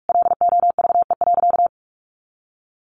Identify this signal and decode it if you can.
Morse code, heard as a keyed tone, about 44 wpm, 700 Hz. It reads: LOVE.
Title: Sentences from Top 1200 Words 45wpm